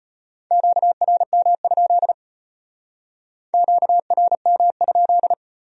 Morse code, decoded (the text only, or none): QRM? QRM?